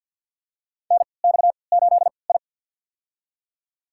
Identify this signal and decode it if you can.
Morse code, heard as a keyed tone, about 48 wpm, spaced as for 28 wpm, 700 Hz. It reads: NX8I
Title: USA Call Signs - 4 Characters - 3x Character Spacing - 50wpm